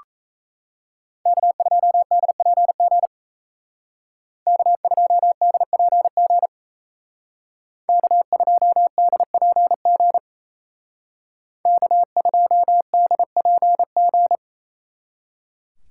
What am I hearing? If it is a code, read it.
Morse code, heard as a keyed tone, about 37 wpm, 700 Hz. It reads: K2DPG K2DPG K2DPG K2DPG